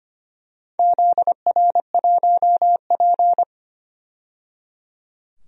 Morse code, heard as a keyed tone, about 25 wpm, 700 Hz. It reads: ZR1P
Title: DXCC-100 Call Signs - 4 Characters - 25wpm